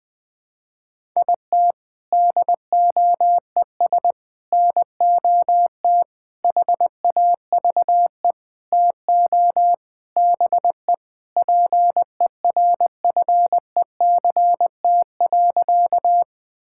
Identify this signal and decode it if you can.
Morse code, heard as a keyed tone, about 21 wpm, 700 Hz. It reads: IT DOES NOT HAVE TO BE PERFECT.
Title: Sentences with 5-Word Phrases - 20wpm